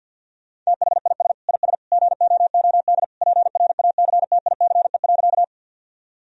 Morse code, heard as a keyed tone, about 52 wpm, 700 Hz. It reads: THIS IS GOOD PRACTICE.
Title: Sentences from Top 1000 Words - Review Part 4 - 50wpm